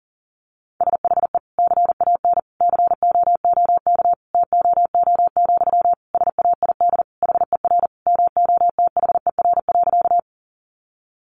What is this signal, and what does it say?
Morse code, heard as a keyed tone, about 40 wpm, 700 Hz. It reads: SHE CAN COOK TOO, SAID HER MOTHER.
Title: Sentences from Top 700 Words 40wpm